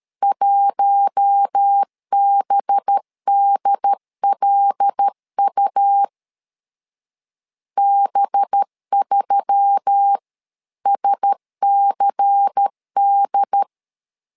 Morse code, heard as a keyed tone, about 13 wpm, 800 Hz. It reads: 1BDLU B3SCD
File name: LEZIONI / 5 LBVC / 5 LBVC 12wpm 1955